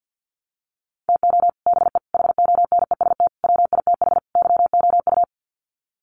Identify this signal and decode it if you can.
Morse code, heard as a keyed tone, about 50 wpm, 700 Hz. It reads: TO BE HONEST WITH YOU
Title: Sets of 5 Words Encore 50wpm